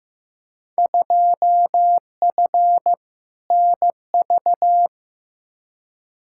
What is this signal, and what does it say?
Morse code, heard as a keyed tone, about 15 wpm, 700 Hz. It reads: 2F NV
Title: Field Day Exchanges - 15wpm